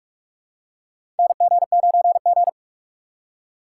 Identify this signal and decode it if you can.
Morse code, heard as a keyed tone, about 41 wpm, 700 Hz. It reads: NG9G
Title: USA Call Signs - 3x Character Spacing - 45wpm